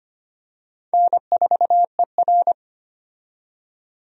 Morse code, heard as a keyed tone, about 25 wpm, 700 Hz. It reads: N4ER
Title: USA Call Signs - 4 Characters - 25wpm